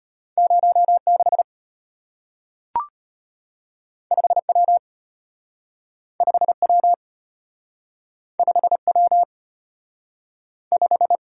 Morse code, heard as a keyed tone, about 34 wpm, 700 Hz. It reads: 06 5W 5W 5W 5